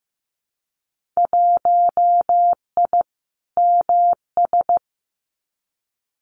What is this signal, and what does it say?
Morse code, heard as a keyed tone, about 15 wpm, 700 Hz. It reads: 1I MS
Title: Winter Field Day Exchanges Encore 15wpm